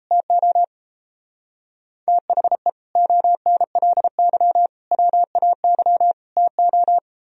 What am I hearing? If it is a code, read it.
Morse code, heard as a keyed tone, about 32 wpm, 700 Hz. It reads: TO THE ONLY WAY TO
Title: Speed-Racing - Sets of 4 Words 28wpm